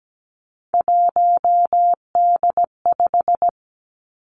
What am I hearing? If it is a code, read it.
Morse code, heard as a keyed tone, about 17 wpm, 700 Hz. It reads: 1D5
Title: Three Challenging Letter-Number Combinations 17wpm